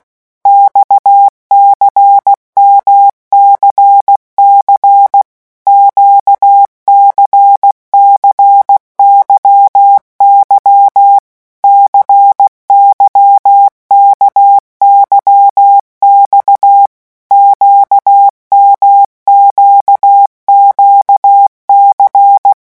Morse code, heard as a keyed tone, about 16 wpm, 800 Hz. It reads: XCMCC QCCYY CYKYX QMQQC
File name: morse mp3 / revision1-vitesse 15 mots